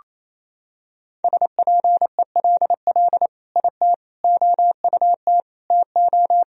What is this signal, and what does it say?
Morse code, heard as a keyed tone, about 28 wpm, 700 Hz. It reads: SPELL IT OUT TO
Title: Sentences from Top 1000 Words - Review Part 1 - 28wpm